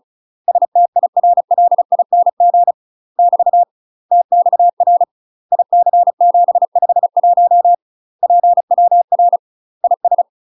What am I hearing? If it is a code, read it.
Morse code, heard as a keyed tone, about 34 wpm, 700 Hz. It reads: STIRLING <BT> TXR IC751 PWR IS
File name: UK MASTER QSOs TRAINING / 35WPM / UK MASTER QSOs 0025